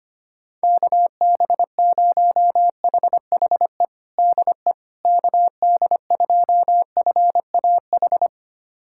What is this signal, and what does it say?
Morse code, heard as a keyed tone, about 25 wpm, 700 Hz. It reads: KB0HHE DE KD2FAH
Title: USA Callsign DE Callsign 25wpm